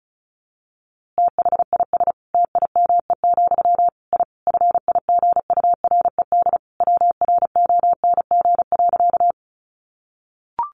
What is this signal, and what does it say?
Morse code, heard as a keyed tone, about 34 wpm, 700 Hz. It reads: THIS TIME, I FIGURED WRONG.
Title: A Sound Of Thunder 35wpm